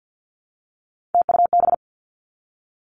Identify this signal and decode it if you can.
Morse code, heard as a keyed tone, about 50 wpm, 700 Hz. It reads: TUB
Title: Three Challenging Letter-Number Combinations 50wpm